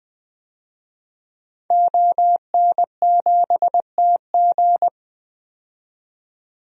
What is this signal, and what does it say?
Morse code, heard as a keyed tone, about 21 wpm, 700 Hz. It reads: ON7TG
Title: DXCC-100 Call Signs - 5 Characters - 20wpm